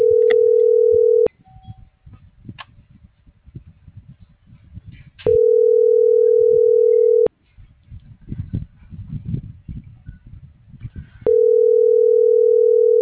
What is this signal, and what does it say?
Ringback tone (North America), 440 + 480 Hz.